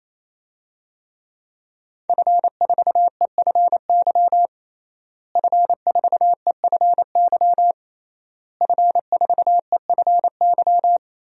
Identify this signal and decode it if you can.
Morse code, heard as a keyed tone, about 28 wpm, 700 Hz. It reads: F4EFY F4EFY F4EFY
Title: DXCC-100 Call Signs - 5 Characters - Repeated 3x - 28wpm